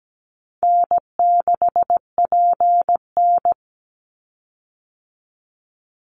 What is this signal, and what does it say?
Morse code, heard as a keyed tone, about 17 wpm, 700 Hz. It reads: N6PN
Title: Easy Call Signs Repeated 2x - 17wpm